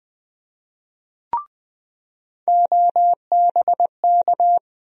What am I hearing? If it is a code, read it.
Morse code, heard as a keyed tone, about 20 wpm, 700 Hz. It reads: OBK
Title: USA Call Sign Postfixes - 3 Letters 20wpm